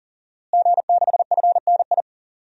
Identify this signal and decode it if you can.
Morse code, heard as a keyed tone, about 40 wpm, 700 Hz. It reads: G6FNI
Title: DXCC-100 Call Signs - 5 Characters - Repeated 3x - 40wpm